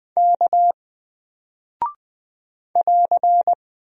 Morse code, heard as a keyed tone, about 20 wpm, 700 Hz. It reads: K <AR>
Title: Single Character (Letter:Number:Punctuation:Prosign) Alt 20wpm